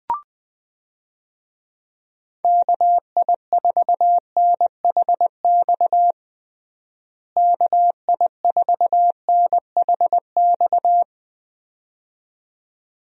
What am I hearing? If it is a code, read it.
Morse code, heard as a keyed tone, about 20 wpm, 700 Hz. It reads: KI4NHX KI4NHX
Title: USA Call Signs - 6 Characters - Repeated 2x - 20wpm